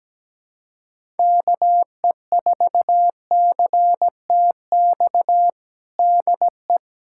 Morse code, heard as a keyed tone, about 17 wpm, 700 Hz. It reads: KE4CTX DE
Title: USA Callsign DE Callsign - 6 Characters - 17wpm